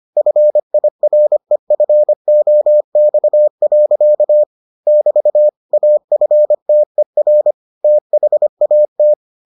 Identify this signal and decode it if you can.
Morse code, heard as a keyed tone, about 25 wpm, 600 Hz. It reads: FIREFOX. <BT> AFTER THAT